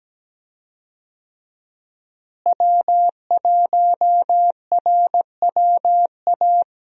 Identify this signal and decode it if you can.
Morse code, heard as a keyed tone, about 17 wpm, 700 Hz. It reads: W1RWA
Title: USA Call Signs - 5 Characters - Repeated 2x - 17wpm